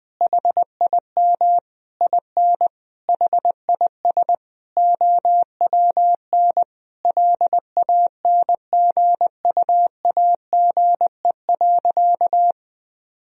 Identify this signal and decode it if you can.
Morse code, heard as a keyed tone, about 20 wpm, 700 Hz. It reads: HIM IN HIS OWN LANGUAGE.
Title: Sentences from Top 1000 Words - Review Part 3 - 20wpm